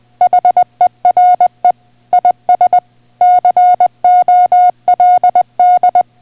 Morse code, heard as a keyed tone, about 20 wpm, 720 Hz. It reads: HERE IS COLD